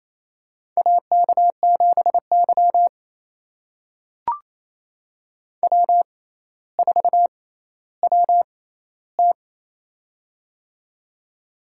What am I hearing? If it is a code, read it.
Morse code, heard as a keyed tone, about 28 wpm, spaced as for 14 wpm, 700 Hz. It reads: AK7Y W4WT
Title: USA Call Signs - 4 Characters - 6x Character Spacing - 28wpm